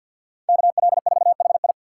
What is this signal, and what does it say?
Morse code, heard as a keyed tone, about 52 wpm, 700 Hz. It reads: KL4HI
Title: USA Call Signs - 5 Characters - 4x Character Spacing - 50wpm